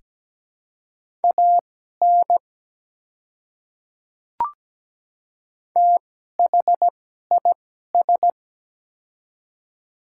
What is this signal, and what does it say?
Morse code, heard as a keyed tone, about 17 wpm, spaced as for 12 wpm, 700 Hz. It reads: AN THIS
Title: 2x Character Spacing - Top 100 Words - 17wpm